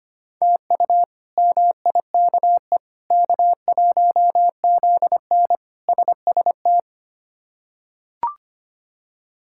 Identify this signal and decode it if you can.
Morse code, heard as a keyed tone, about 25 wpm, 700 Hz. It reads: TU MIKE K1ZN SST
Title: K1USN SST QSO Practice - 25wpm